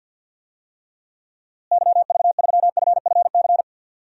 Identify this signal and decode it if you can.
Morse code, heard as a keyed tone, about 50 wpm, 700 Hz. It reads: YV3FFC
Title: DXCC-100 Call Signs - 6 Characters - Repeated 3x - 50wpm